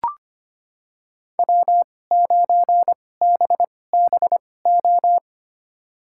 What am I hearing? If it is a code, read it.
Morse code, heard as a keyed tone, about 25 wpm, spaced as for 18 wpm, 700 Hz. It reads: W9BBO
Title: USA Call Signs - 5 Characters - 2x Character Spacing - 25wpm